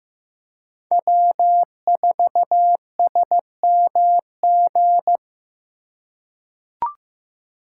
Morse code, heard as a keyed tone, about 15 wpm, 700 Hz. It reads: W4SMG
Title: USA Call Signs - 5 Characters - 4x Character Spacing - 15wpm